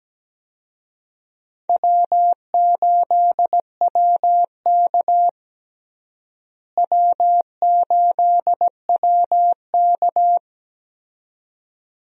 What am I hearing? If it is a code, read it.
Morse code, heard as a keyed tone, about 17 wpm, 700 Hz. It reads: W8WK W8WK
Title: USA Call Signs - 4 Characters - Repeated 2x - 17wpm